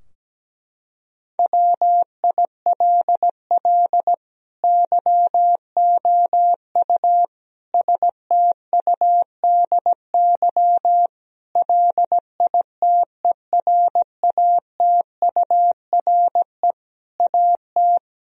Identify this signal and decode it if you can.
Morse code, heard as a keyed tone, about 17 wpm, 700 Hz. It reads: WILL YOU STUDY LITERATURE AT
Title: Sentences from Top 1000 Words - Review Part 3 - 17wpm